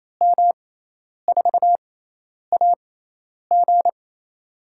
Morse code, heard as a keyed tone, about 27 wpm, spaced as for 10 wpm, 700 Hz. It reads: M4AG
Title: USA Call Signs - 5 Characters - 6x Character Spacing - 28wpm